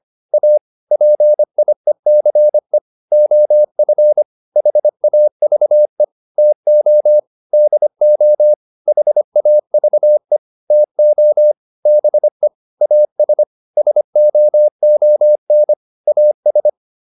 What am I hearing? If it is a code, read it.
Morse code, heard as a keyed tone, about 25 wpm, 600 Hz. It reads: A PIECE OF HAVE TO DO HAVE TO BE AS SOON AS
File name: WD204a 25